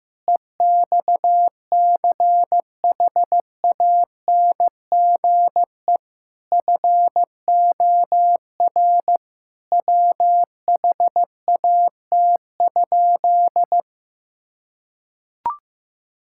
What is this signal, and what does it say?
Morse code, heard as a keyed tone, about 15 wpm, 700 Hz. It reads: EXCHANGE FOR WHAT?
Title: Sentences from Top 1000 Words - Review Part 6 - 15wpm